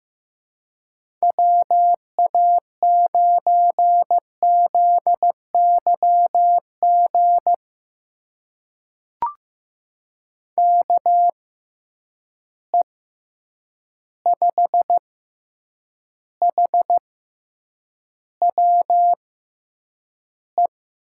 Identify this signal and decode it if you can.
Morse code, heard as a keyed tone, about 15 wpm, spaced as for 8 wpm, 700 Hz. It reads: WA9ZYG KE5HWE
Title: USA Call Signs - 6 Characters - 6x Character Spacing - 15wpm